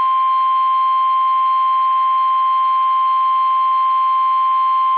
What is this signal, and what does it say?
Emergency Alert System (SAME) tones (the 1050 Hz attention tone). It is heard from 0.6 to 4.7 s.
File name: eas / KUSO / KUSO